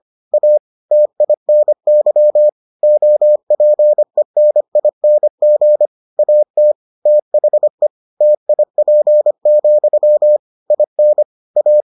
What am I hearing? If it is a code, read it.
Morse code, heard as a keyed tone, about 25 wpm, 600 Hz. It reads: A TINY OPENING AT THE TIP, IN A